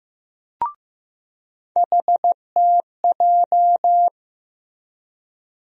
Morse code, heard as a keyed tone, about 15 wpm, 700 Hz. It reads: HTJ
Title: USA Call Sign Postfixes - 3 Letters 15wpm